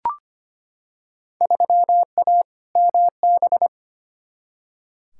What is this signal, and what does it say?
Morse code, heard as a keyed tone, about 25 wpm, 700 Hz. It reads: 3A MB
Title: Field Day Exchanges Encore - 25wpm